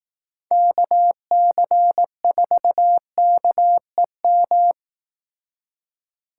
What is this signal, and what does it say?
Morse code, heard as a keyed tone, about 18 wpm, 700 Hz. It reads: KC4KEM
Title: Speed Racing - USA 2x3 Call Signs 15wpm